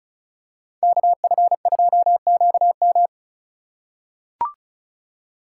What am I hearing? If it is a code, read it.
Morse code, heard as a keyed tone, about 34 wpm, 700 Hz. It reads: KF2QM
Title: USA Call Signs - 5 Characters - 2x Character Spacing - 35wpm